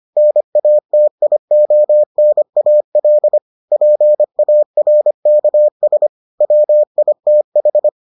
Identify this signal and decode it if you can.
Morse code, heard as a keyed tone, about 25 wpm, 600 Hz. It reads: NATIONAL PARKS WITH